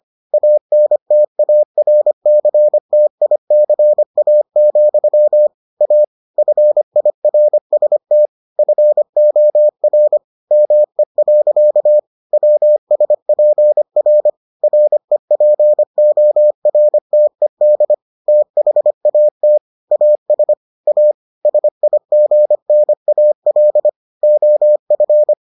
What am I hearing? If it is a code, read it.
Morse code, heard as a keyed tone, about 25 wpm, 600 Hz. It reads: ANTARCTICA, A FIRST FOR ME. WSPR REPORTED THAT AS A SIGNAL OF